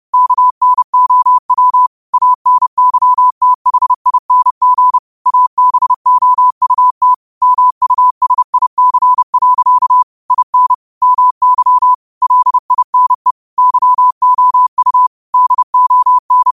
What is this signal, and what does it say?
Morse code, heard as a keyed tone, about 30 wpm, 1000 Hz. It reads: KNOW ANYTHING ABOUT MUSIC. IN MY LINE YOU DON